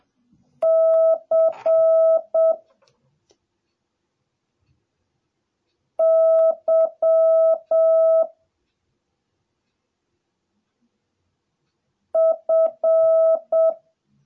Morse code, heard as a keyed tone, about 7 wpm, 650 Hz. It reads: C Y F